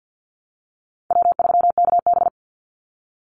Morse code, heard as a keyed tone, about 50 wpm, 700 Hz. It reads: W3KB
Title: Easy Call Signs - 50wpm